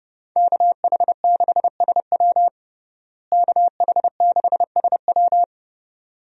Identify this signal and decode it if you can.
Morse code, heard as a keyed tone, about 29 wpm, 700 Hz. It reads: KH6SW KH6SW
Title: DXCC-100 Call Signs - 5 Characters - Repeated 2x - 30wpm